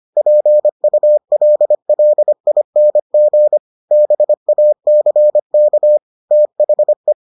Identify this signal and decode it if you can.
Morse code, heard as a keyed tone, about 25 wpm, 600 Hz. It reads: PULLING BACK THE